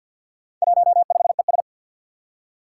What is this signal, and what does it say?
Morse code, heard as a keyed tone, about 51 wpm, 700 Hz. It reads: 15ES
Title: Four Challenging Letter-Number Combinations 50wpm